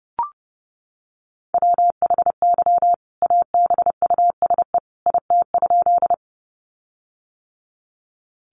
Morse code, heard as a keyed tone, about 30 wpm, 700 Hz. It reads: WHY ABUSE IT?
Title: Sentences from Top 1300 Words 30wpm